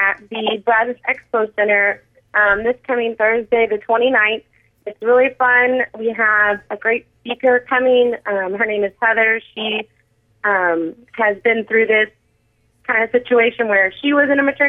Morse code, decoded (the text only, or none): none